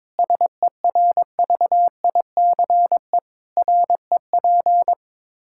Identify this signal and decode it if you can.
Morse code, heard as a keyed tone, about 22 wpm, 700 Hz. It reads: SERVICE REP
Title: Practice QSO - Exchange 3 - More Info - 22wpm